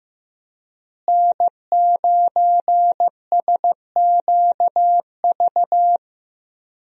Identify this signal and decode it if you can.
Morse code, heard as a keyed tone, about 15 wpm, 700 Hz. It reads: N9SQV
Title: USA Call Signs - 5 Characters - Repeated 2x - 15wpm